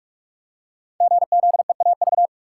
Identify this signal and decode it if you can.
Morse code, heard as a keyed tone, about 44 wpm, 690 Hz. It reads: GZEAV
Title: Code Groups - 5 Letter:Numbers - 45wpm